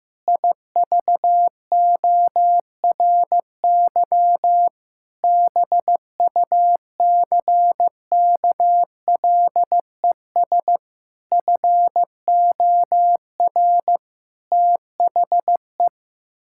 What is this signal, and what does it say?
Morse code, heard as a keyed tone, about 15 wpm, 700 Hz. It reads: IVORY BUCKLES FOR THE
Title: Pangram Challenge v2 - 15wpm